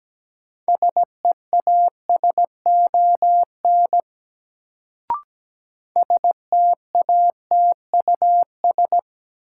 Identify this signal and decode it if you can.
Morse code, heard as a keyed tone, about 17 wpm, 700 Hz. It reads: SEASON STATUS
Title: Sentences from Top 1400 Words - New Word Drill - 17wpm